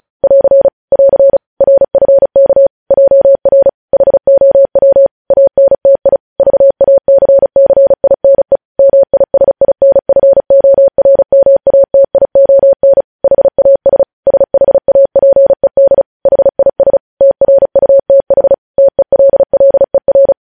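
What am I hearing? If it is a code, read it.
Morse code, heard as a keyed tone, about 36 wpm, 550 Hz. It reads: <AR> <AR> RFK JR HOW ANTI VACCINE MISINFORMATION HAS SHAPED HIS TRUTH TELLER
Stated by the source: Morse code news headlines sent at 35.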